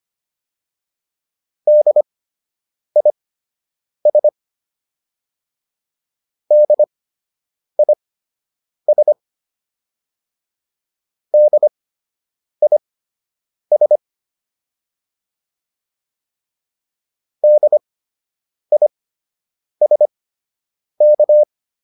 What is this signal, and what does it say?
Morse code, heard as a keyed tone, about 25 wpm, spaced as for 8 wpm, 600 Hz. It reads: DIS DIS DIS DISK